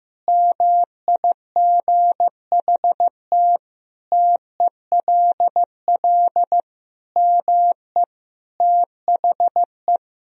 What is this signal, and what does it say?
Morse code, heard as a keyed tone, about 15 wpm, 700 Hz. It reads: MIGHT TELL ME THE
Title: Sentences from Top 1000 Words - Review Part 1 - 15wpm